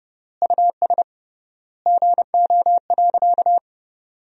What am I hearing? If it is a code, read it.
Morse code, heard as a keyed tone, about 30 wpm, 700 Hz. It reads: US GO.
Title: 3x Word-Spacing - Sentences from Top 400 Words - 30wpm